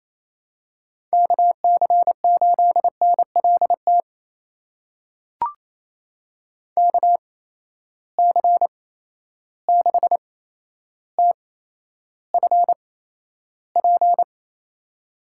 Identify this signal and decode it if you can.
Morse code, heard as a keyed tone, about 28 wpm, spaced as for 12 wpm, 700 Hz. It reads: KC8NLT KC6TFP
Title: USA Call Signs - 6 Characters - 8x Character Spacing - 28wpm